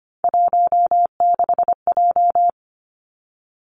Morse code, heard as a keyed tone, about 25 wpm, 700 Hz. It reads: 16J
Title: Three Challenging Letter-Number Combinations 25wpm